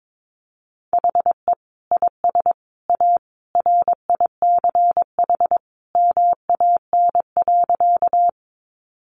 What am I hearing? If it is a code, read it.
Morse code, heard as a keyed tone, about 22 wpm, 700 Hz. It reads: HE IS A RICH MAN.
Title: Sentences from Top 1100 Words 22wpm